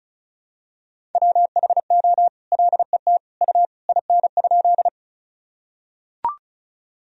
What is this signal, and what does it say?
Morse code, heard as a keyed tone, about 34 wpm, 700 Hz. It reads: WHO LET U IN?
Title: Sentences from Top 300 Words and Abbreviations 35wpm